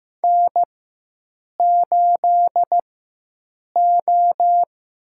Morse code, heard as a keyed tone, about 16 wpm, 700 Hz. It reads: N 8 O
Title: USA Call Signs - 4 Characters - 4x Character Spacing - 15wpm